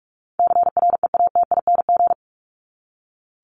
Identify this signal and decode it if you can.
Morse code, heard as a keyed tone, about 44 wpm, 700 Hz. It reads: CREATING
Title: ING Suffix - Top 100 Words with Base Words 45wpm